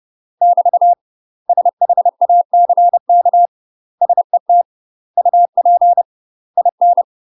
Morse code, heard as a keyed tone, about 30 wpm, 700 Hz. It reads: <BT> SHACK SET UP IN
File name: UK MASTER QSOs TRAINING/30WPM